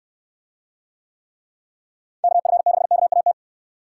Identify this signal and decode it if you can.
Morse code, heard as a keyed tone, about 67 wpm, 700 Hz. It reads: KF6KAT